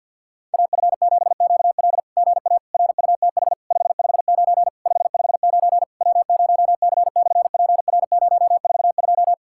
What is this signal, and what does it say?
Morse code, heard as a keyed tone, about 50 wpm, 700 Hz. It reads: AF7QL GA RUTH 559 559 W0C/PR042
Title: SOTA QSO Practice 50wpm